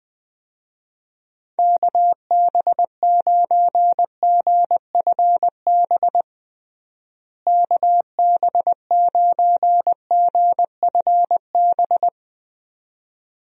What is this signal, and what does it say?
Morse code, heard as a keyed tone, about 20 wpm, 700 Hz. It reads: KB9GFB KB9GFB
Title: USA Call Signs - 6 Characters - Repeated 2x - 20wpm